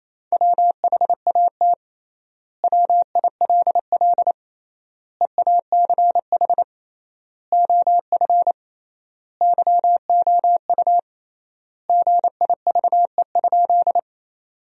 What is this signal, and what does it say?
Morse code, heard as a keyed tone, about 28 wpm, 700 Hz. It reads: WHAT WILL EACH OF YOU GIVE?
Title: 3x Word-Spacing - Sentences from Top 200 Words - 28wpm